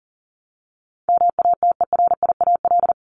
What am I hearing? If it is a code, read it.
Morse code, heard as a keyed tone, about 40 wpm, 700 Hz. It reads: MATERIAL
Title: ES:S Suffix - Top 100 Words with Base Words 40wpm